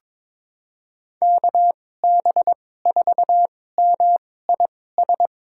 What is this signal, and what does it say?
Morse code, heard as a keyed tone, about 21 wpm, spaced as for 16 wpm, 700 Hz. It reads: KB4MIS
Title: USA Call Signs - 6 Characters - 2x Character Spacing - 22wpm